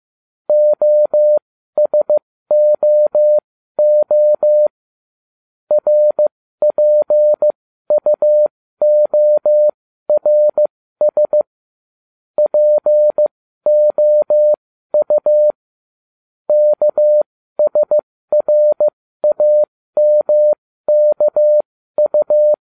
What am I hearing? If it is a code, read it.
Morse code, heard as a keyed tone, about 15 wpm, spaced as for 12 wpm, 600 Hz. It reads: OSOO RPUORS POU KSRAMKU